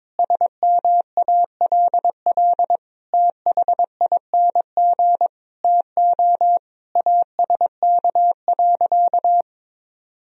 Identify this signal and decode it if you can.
Morse code, heard as a keyed tone, about 21 wpm, 700 Hz. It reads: SMALL THING TO ASK.
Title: Sentences from Top 1000 Words - Review Part 1 - 22wpm